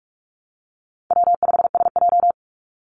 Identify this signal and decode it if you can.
Morse code, heard as a keyed tone, about 44 wpm, 700 Hz. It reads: W5SJ
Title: Easy Call Signs Repeated 2x - 45wpm